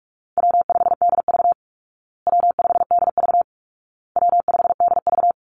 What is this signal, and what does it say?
Morse code, heard as a keyed tone, about 44 wpm, 700 Hz. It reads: W5DV W5DV W5DV
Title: Call Signs Repeated 2x - 45wpm